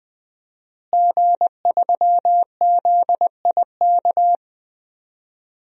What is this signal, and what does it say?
Morse code, heard as a keyed tone, about 20 wpm, 700 Hz. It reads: G3ZIK
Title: DXCC-100 Call Signs - 5 Characters - Repeated 2x - 20wpm